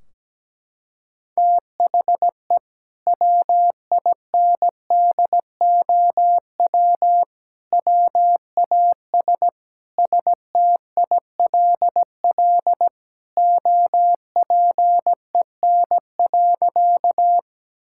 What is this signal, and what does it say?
Morse code, heard as a keyed tone, about 17 wpm, 700 Hz. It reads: THE WINDOW WAS STILL OPEN.
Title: Sentences from Top 1600 Words 17wpm